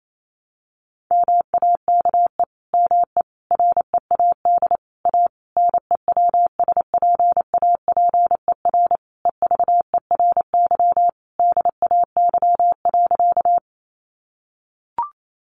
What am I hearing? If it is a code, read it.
Morse code, heard as a keyed tone, about 28 wpm, 700 Hz. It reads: MAKE ME READ A NEWSPAPER EVERY DAY.
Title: Sentences from Top 1100 Words 28wpm